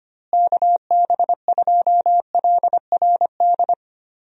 Morse code, heard as a keyed tone, about 25 wpm, 700 Hz. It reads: KB2LRD
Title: USA Call Signs - 6 Characters - Repeated 3x - 25wpm